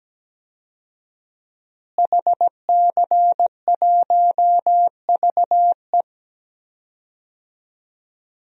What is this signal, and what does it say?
Morse code, heard as a keyed tone, about 17 wpm, 700 Hz. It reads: HC1VE
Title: DXCC-100 Call Signs - 5 Characters - Repeated 2x - 17wpm